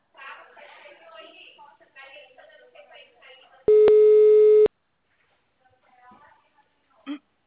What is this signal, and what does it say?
Ringback tone (Europe, 425 Hz), with speech.
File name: out- -2019- - - .23